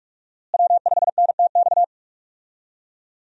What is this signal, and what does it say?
Morse code, heard as a keyed tone, about 47 wpm, 700 Hz. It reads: W5NTX
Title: USA Call Signs - 5 Characters - Repeated 2x - 45wpm